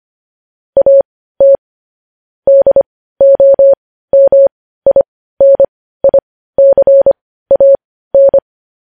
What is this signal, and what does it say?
Morse code, heard as a keyed tone, about 25 wpm, spaced as for 15 wpm, 550 Hz. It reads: AT DOMINICAN
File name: podcast / News headlines-1x-25x15